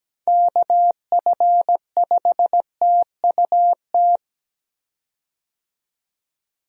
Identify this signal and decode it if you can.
Morse code, heard as a keyed tone, about 17 wpm, 700 Hz. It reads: KF5TUT
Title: USA Call Signs - 6 Characters - 17wpm